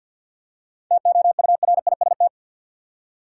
Morse code, heard as a keyed tone, about 52 wpm, 700 Hz. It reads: TOURIST
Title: Seven Letter Words - 50wpm